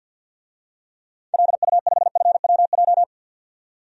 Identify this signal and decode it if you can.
Morse code, heard as a keyed tone, about 52 wpm, 700 Hz. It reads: LU5FWJ
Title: DXCC-100 Call Signs - 6 Characters - 50wpm